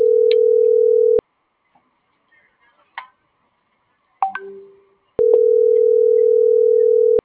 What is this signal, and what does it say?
Ringback tone (North America), 440 + 480 Hz.